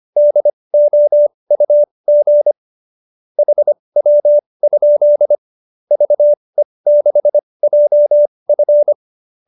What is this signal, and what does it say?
Morse code, heard as a keyed tone, about 25 wpm, spaced as for 20 wpm, 590 Hz. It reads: DOUG HW? VE6JF
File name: QSO303 20